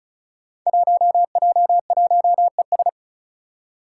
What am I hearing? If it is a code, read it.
Morse code, heard as a keyed tone, about 36 wpm, 700 Hz. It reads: 1J1ES
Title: Five Challenging Letter-Number Combinations 35wpm